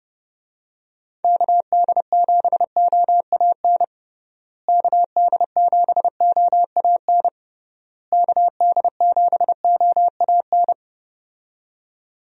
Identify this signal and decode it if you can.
Morse code, heard as a keyed tone, about 30 wpm, 700 Hz. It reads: KD7OAN KD7OAN KD7OAN
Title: USA Call Signs - 6 Characters - Repeated 3x - 30wpm